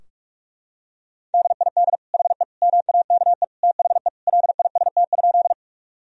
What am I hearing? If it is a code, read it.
Morse code, heard as a keyed tone, about 41 wpm, 700 Hz. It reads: DID HE MAKE THE LIST?
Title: Sentences from Top 1000 Words - Review Part 2 - 45wpm